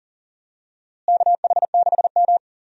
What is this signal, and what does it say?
Morse code, heard as a keyed tone, about 38 wpm, 700 Hz. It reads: KH6M
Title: USA Call Signs - 4 Characters - Repeated 2x - 40wpm